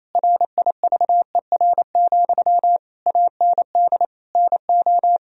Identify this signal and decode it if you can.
Morse code, heard as a keyed tone, about 28 wpm, 700 Hz. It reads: RIVER, AND NO
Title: Sentences from Top 1000 Words - Review Part 1 - 28wpm